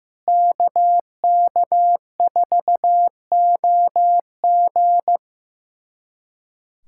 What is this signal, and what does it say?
Morse code, heard as a keyed tone, about 16 wpm, 700 Hz. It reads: KK4OG
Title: USA Callsign DE Callsign - 5 Characters - 15wpm